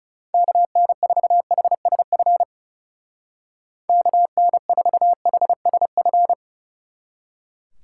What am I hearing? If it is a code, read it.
Morse code, heard as a keyed tone, about 36 wpm, 700 Hz. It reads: KN4HSF KN4HSF
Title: Speed Racing - USA 2x3 Call Signs 30wpm